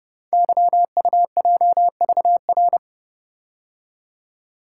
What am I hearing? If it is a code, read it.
Morse code, heard as a keyed tone, about 30 wpm, 700 Hz. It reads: YUJVR
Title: Code Groups - 5 Letters - 30wpm